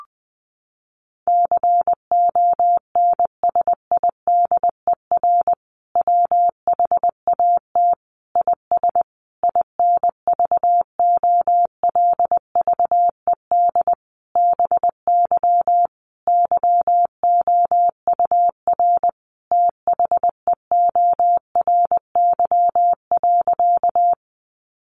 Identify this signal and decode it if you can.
Morse code, heard as a keyed tone, about 20 wpm, 700 Hz. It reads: CONSIDER WHAT IS INVOLVED BY YOUR THEORY.
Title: Sentences from Top 800 Words 20wpm